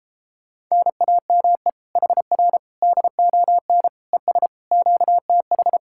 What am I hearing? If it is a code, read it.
Morse code, heard as a keyed tone, about 33 wpm, 700 Hz. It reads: NAME HR DON ES QTH